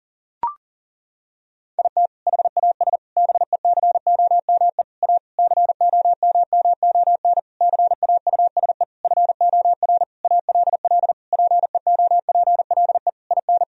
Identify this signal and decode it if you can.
Morse code, heard as a keyed tone, about 40 wpm, 700 Hz. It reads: IT HAS BECOME A COMMON CAUSE FOR ALL PEOPLE IN
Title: Sentences from Top 1000 Words - Review Part 3 - 40wpm